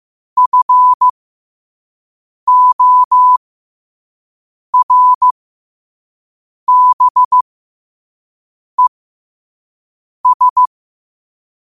Morse code, heard as a keyed tone, about 15 wpm, spaced as for 5 wpm, 1000 Hz. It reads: FORBES
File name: qotdfiles / 5wpm